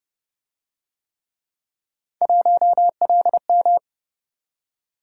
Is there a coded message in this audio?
Yes: Morse code from 2.2 to 3.8 s, heard as a keyed tone, about 29 wpm, 700 Hz. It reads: 1LM